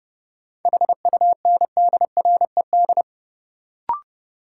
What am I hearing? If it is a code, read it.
Morse code, heard as a keyed tone, about 30 wpm, 700 Hz. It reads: HUNDRED
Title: Sentences from Top 400 Words - New Word Drill - 30wpm